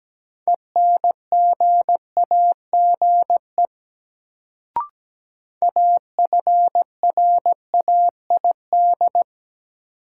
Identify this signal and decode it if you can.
Morse code, heard as a keyed tone, about 17 wpm, 700 Hz. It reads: ENGAGE AFRAID
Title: Sentences from Top 1500 Words - New Word Drill Alt - 17wpm